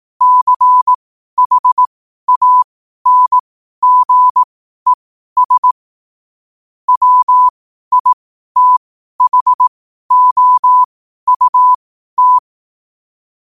Morse code, heard as a keyed tone, about 18 wpm, spaced as for 12 wpm, 1000 Hz. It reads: CHANGES WITHOUT
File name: qotdfiles / 12wpm